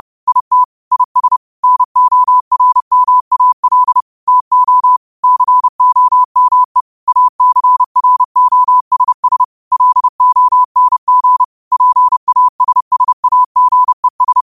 Morse code, heard as a keyed tone, about 30 wpm, 1000 Hz. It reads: IT IS NORMAL TO COME ACROSS LONG PASSAGES
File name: qotdfiles / 30wpm